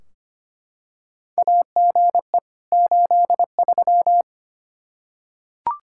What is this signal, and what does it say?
Morse code, heard as a keyed tone, about 25 wpm, 700 Hz. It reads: AGE 83
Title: Practice QSO - Exchange 3 - More Info - 25wpm